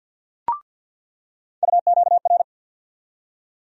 Morse code, heard as a keyed tone, about 47 wpm, 700 Hz. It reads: UQR
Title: USA Call Sign Postfixes - 3 Letters 50wpm